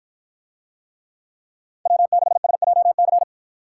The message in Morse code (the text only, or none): W6SJC